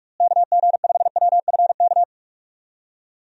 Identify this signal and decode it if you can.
Morse code, heard as a keyed tone, about 45 wpm, 700 Hz. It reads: KG5WFK
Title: USA Call Signs - 6 Characters - Repeated 2x - 45wpm